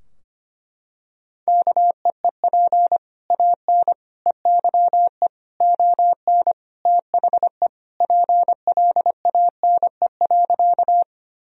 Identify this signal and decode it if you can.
Morse code, heard as a keyed tone, about 25 wpm, 700 Hz. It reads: KEEP AN EYE ON THE PLANE.
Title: Sentences from Top 1000 Words - Review Part 3 - 25wpm